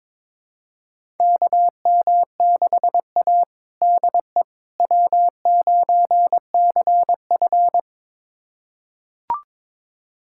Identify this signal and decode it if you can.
Morse code, heard as a keyed tone, about 22 wpm, 700 Hz. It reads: KM6A DE W9CF
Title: USA Callsign DE Callsign - 4 Characters - 22wpm